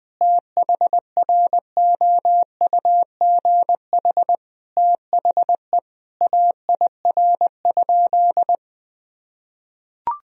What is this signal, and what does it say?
Morse code, heard as a keyed tone, about 20 wpm, 700 Hz. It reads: THROUGH THE AIR?
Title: Sentences from Top 1000 Words - Review Part 2 - 20wpm